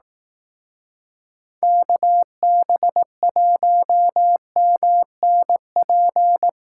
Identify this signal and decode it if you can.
Morse code, heard as a keyed tone, about 18 wpm, 700 Hz. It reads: KB1MNP
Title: Speed Racing - USA 2x3 Call Signs 15wpm